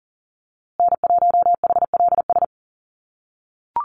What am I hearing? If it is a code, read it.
Morse code, heard as a keyed tone, about 40 wpm, 700 Hz. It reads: N1HLS
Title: USA Call Signs Repeated 3x 40wpm